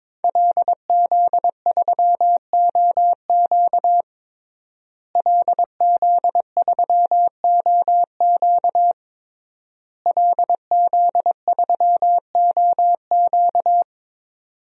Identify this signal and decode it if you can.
Morse code, heard as a keyed tone, about 22 wpm, 700 Hz. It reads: LZ3OQ LZ3OQ LZ3OQ
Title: DXCC-100 Call Signs - 5 Characters - Repeated 3x - 22wpm